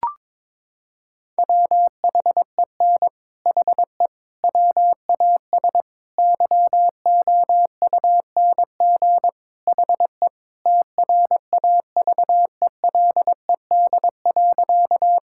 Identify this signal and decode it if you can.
Morse code, heard as a keyed tone, about 22 wpm, 700 Hz. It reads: WHEN HE WAS YOUNG HE TRAVELED.
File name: practice / 22wpm / Sentences with 3-Word Phrases - 22wpm